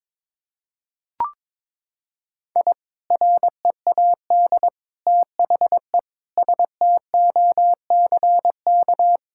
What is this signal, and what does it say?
Morse code, heard as a keyed tone, about 22 wpm, 700 Hz. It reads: I READ THE STOCK
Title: Sentences from Top 1000 Words - Review Part 6 - 22wpm